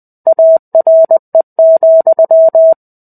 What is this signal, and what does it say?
Morse code, heard as a keyed tone, about 20 wpm, 650 Hz. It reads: ARE,
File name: 05-Chapter-05 20 WPM